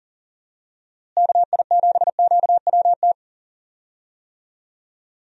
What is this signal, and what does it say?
Morse code, heard as a keyed tone, about 40 wpm, 700 Hz. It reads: KI7QWT
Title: USA Call Signs - 6 Characters - Repeated 2x - 40wpm